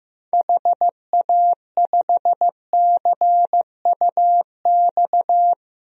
Morse code, heard as a keyed tone, about 15 wpm, 700 Hz. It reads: HA5CUX
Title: DXCC-100 Call Signs - 6 Characters - Repeated 2x - 15wpm